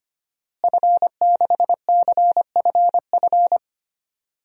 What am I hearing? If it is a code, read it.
Morse code, heard as a keyed tone, about 25 wpm, 700 Hz. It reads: F6CFF
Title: DXCC-100 Call Signs - 5 Characters - Repeated 3x - 25wpm